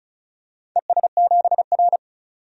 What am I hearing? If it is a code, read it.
Morse code, heard as a keyed tone, about 36 wpm, 700 Hz. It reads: ES7R
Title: DXCC-100 Call Signs - 4 Characters - 35wpm